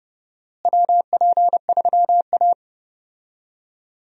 Morse code, heard as a keyed tone, about 31 wpm, 700 Hz. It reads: WP3A
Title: USA Call Signs - 4 Characters - Repeated 2x - 30wpm